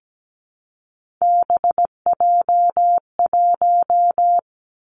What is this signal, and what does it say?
Morse code, heard as a keyed tone, about 17 wpm, 700 Hz. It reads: BJ1
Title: Three Challenging Letter-Number Combinations 17wpm